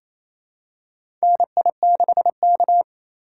Morse code, heard as a keyed tone, about 28 wpm, 700 Hz. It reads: NI6K
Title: USA Call Signs - 4 Characters - 8x Character Spacing - 28wpm